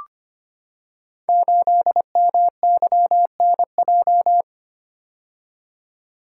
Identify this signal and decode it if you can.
Morse code, heard as a keyed tone, about 25 wpm, 700 Hz. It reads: 8MYNJ
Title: Code Groups - 5 Letter:Numbers - 25wpm